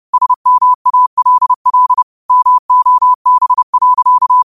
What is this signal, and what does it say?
Morse code, heard as a keyed tone, about 30 wpm, 1000 Hz. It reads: SMALL MOB.
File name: qotdfiles / 30wpm